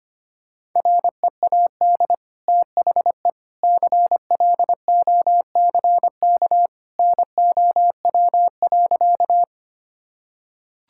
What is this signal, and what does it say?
Morse code, heard as a keyed tone, about 25 wpm, 690 Hz. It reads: READ THE CLOCK NOW.
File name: Easy Sentences with Hints - 25wpm